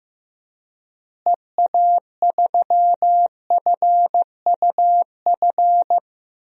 Morse code, heard as a keyed tone, about 15 wpm, 700 Hz. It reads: EA3FUF
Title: DXCC-100 Call Signs - 6 Characters - Repeated 3x - 15wpm